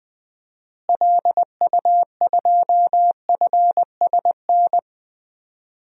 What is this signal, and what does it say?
Morse code, heard as a keyed tone, about 21 wpm, 700 Hz. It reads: LU2FSN
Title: DXCC-100 Call Signs - 6 Characters - Repeated 3x - 20wpm